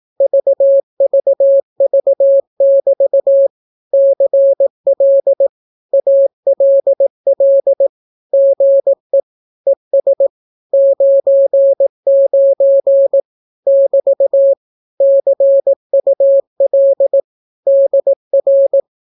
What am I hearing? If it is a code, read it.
Morse code, heard as a keyed tone, about 18 wpm, 550 Hz. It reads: VVV<BT> CL ALL GE ES 99 <BT> CUL DR
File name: naukapodstaw / skroty / skroty